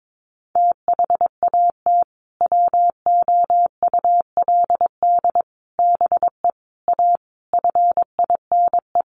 Morse code, heard as a keyed tone, about 22 wpm, 700 Hz. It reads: THAT WOULD BE A FINE
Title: Sentences from Top 500 Words 22wpm